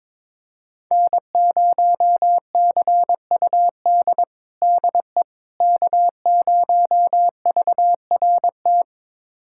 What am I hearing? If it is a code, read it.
Morse code, heard as a keyed tone, about 21 wpm, 700 Hz. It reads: N0CUD DE K0VRT
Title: USA Callsign DE Callsign - 5 Characters - 22wpm